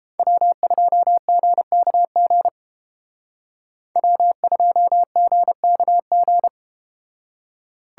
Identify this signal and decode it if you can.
Morse code, heard as a keyed tone, about 32 wpm, 700 Hz. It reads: W2GKG W2GKG